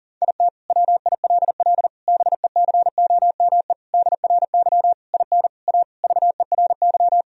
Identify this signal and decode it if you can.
Morse code, heard as a keyed tone, about 40 wpm, 700 Hz. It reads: IT WILL BECOME DRY IN A VERY
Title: Sentences from Top 1000 Words - Review Part 3 - 40wpm